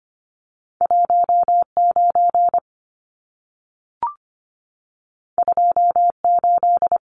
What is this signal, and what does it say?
Morse code, heard as a keyed tone, about 25 wpm, 700 Hz. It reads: 19 28
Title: Numbers - 2-Digit 25wpm